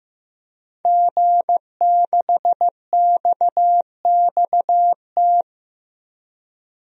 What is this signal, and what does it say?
Morse code, heard as a keyed tone, about 15 wpm, 700 Hz. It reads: G6XXT
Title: DXCC-100 Call Signs - 5 Characters - 15wpm